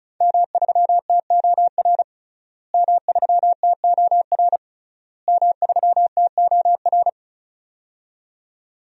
Morse code, heard as a keyed tone, about 36 wpm, 700 Hz. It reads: M3TOR M3TOR M3TOR
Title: DXCC-100 Call Signs - 5 Characters - Repeated 3x - 35wpm